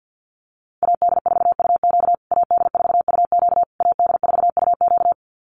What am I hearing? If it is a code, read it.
Morse code, heard as a keyed tone, about 50 wpm, 700 Hz. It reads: AD4UQ AD4UQ AD4UQ
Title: USA Call Signs Repeated 3x 50wpm